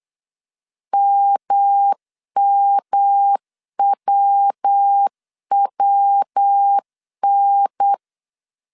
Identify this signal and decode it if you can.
Morse code, heard as a keyed tone, about 8 wpm, 800 Hz. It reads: MMWWN
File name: CW / DET / 1 3 1500C S B 8wpm 4141